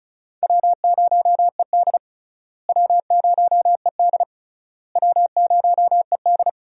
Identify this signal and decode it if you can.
Morse code, heard as a keyed tone, about 36 wpm, 700 Hz. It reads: W0ED W0ED W0ED
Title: USA Call Signs - 4 Characters - Repeated 3x - 35wpm